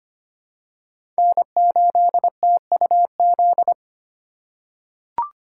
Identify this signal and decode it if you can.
Morse code, heard as a keyed tone, about 25 wpm, 700 Hz. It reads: N8TUZ
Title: USA Call Signs - 2x Character Spacing - 25wpm